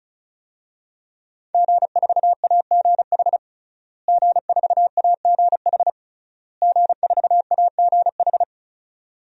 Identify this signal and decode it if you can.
Morse code, heard as a keyed tone, about 34 wpm, 700 Hz. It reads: G4AGH G4AGH G4AGH
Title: DXCC-100 Call Signs - 5 Characters - Repeated 3x - 35wpm